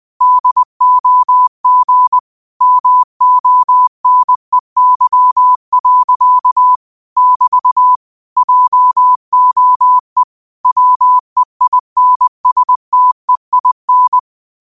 Morse code, heard as a keyed tone, about 20 wpm, 1000 Hz. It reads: DOG MONEY. <BT> JOE WEINSTEIN
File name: qotdfiles / 20wpm